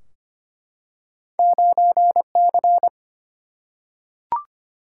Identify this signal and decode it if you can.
Morse code, heard as a keyed tone, about 25 wpm, 700 Hz. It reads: 9C
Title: Single-Letter Single-Number 25wpm